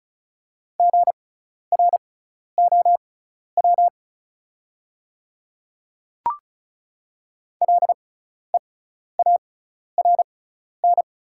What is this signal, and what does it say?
Morse code, heard as a keyed tone, about 36 wpm, spaced as for 12 wpm, 690 Hz. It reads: GROW LEARN
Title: 6x Character Spacing - Top 300 Words - 35wpm